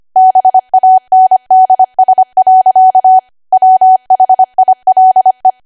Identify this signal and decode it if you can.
Morse code, heard as a keyed tone, about 25 wpm, 750 Hz. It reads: BANDS. WHILE
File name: Archive / 25 WPM / 25WPM